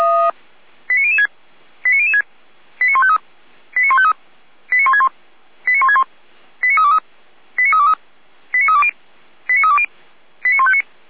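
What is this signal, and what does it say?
Five-tone selective call (ZVEI), heard as quick runs of five tones, tones of about 80 ms. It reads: 89006 89006 89153 89153 89161 89161 89323 89323 89320 89320 89159
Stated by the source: Five-tone ZVEI selective call received by SDR.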